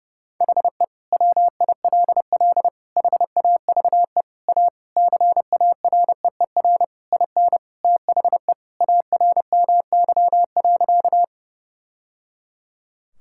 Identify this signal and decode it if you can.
Morse code, heard as a keyed tone, about 32 wpm, 700 Hz. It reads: HE WILL HAVE A CAREER IN THE ARMY.
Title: Sentences from Top 1000 Words - Review Part 6 - 30wpm